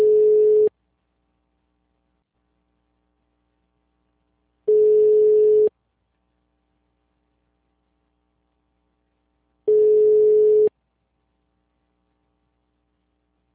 Ringback tone (Europe, 425 Hz).